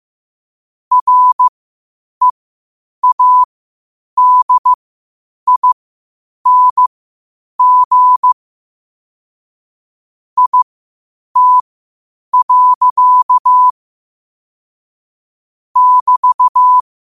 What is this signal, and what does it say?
Morse code, heard as a keyed tone, about 15 wpm, spaced as for 8 wpm, 1000 Hz. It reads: READING IT. <BT>
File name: qotdfiles / 8wpm